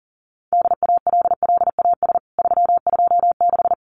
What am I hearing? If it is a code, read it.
Morse code, heard as a keyed tone, about 40 wpm, 700 Hz. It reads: DALLAS 326
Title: CWT Exchange 40wpm